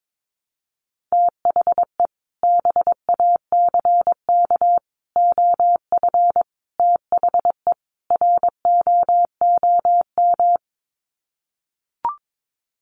Morse code, heard as a keyed tone, about 22 wpm, 700 Hz. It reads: THE BACK OF THE ROOM
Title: Sets of 5 Words Encore 22wpm